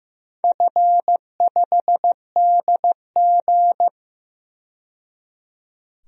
Morse code, heard as a keyed tone, about 16 wpm, 700 Hz. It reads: F5DG
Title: DXCC-100 Call Signs - 4 Characters - 15wpm